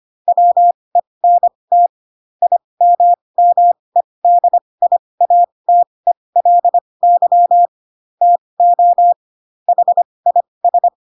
Morse code, heard as a keyed tone, about 25 wpm, spaced as for 20 wpm, 700 Hz. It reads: WENT IMMEDIATELY TO HIS
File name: Pinocchio / Pinocchio ch2 15 25 ik1hge